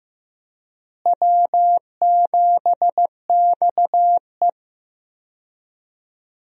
Morse code, heard as a keyed tone, about 15 wpm, 700 Hz. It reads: W7XE
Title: USA Call Signs - 4 Characters - Repeated 3x - 15wpm